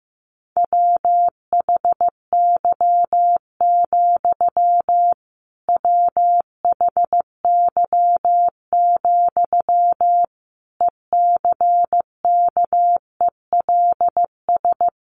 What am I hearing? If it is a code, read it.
Morse code, heard as a keyed tone, about 15 wpm, 700 Hz. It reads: WHY, WHY, ECKELS
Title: A Sound Of Thunder 15wpm